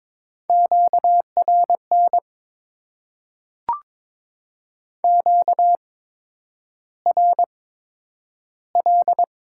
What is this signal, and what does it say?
Morse code, heard as a keyed tone, about 21 wpm, spaced as for 9 wpm, 700 Hz. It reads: QRN QRL
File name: Top 90 QSO Elements - 8x Character Spacing - 22wpm